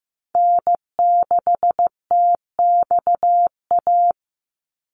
Morse code, heard as a keyed tone, about 15 wpm, 700 Hz. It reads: N6TXA
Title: USA Call Signs 15wpm